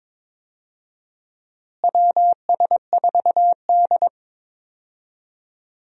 Morse code, heard as a keyed tone, about 22 wpm, 700 Hz. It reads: WS4D
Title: USA Call Signs - 4 Characters - Repeated 2x - 22wpm